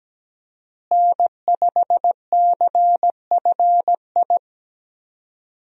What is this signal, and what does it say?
Morse code, heard as a keyed tone, about 17 wpm, 700 Hz. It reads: N5CFI
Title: USA Call Signs - 5 Characters - Repeated 3x - 17wpm